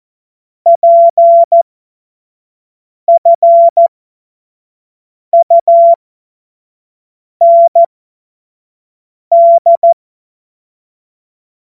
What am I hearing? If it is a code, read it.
Morse code, heard as a keyed tone, about 14 wpm, spaced as for 5 wpm, 680 Hz. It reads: PFUND